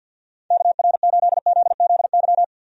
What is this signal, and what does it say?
Morse code, heard as a keyed tone, about 51 wpm, 700 Hz. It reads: KR8ZZY
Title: USA Call Signs - 6 Characters - 50wpm